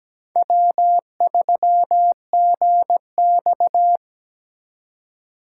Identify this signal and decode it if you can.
Morse code, heard as a keyed tone, about 20 wpm, 700 Hz. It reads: W3GX
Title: Speed Racing - USA 1x2 Call Signs 17wpm